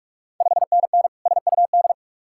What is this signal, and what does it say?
Morse code, heard as a keyed tone, about 45 wpm, 700 Hz. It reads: 5NN SUD
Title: Ontario QSO Party Exchange 45wpm